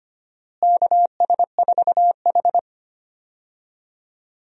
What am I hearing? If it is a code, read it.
Morse code, heard as a keyed tone, about 25 wpm, 700 Hz. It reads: KS4H
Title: USA Call Signs - 4 Characters - Repeated 3x - 25wpm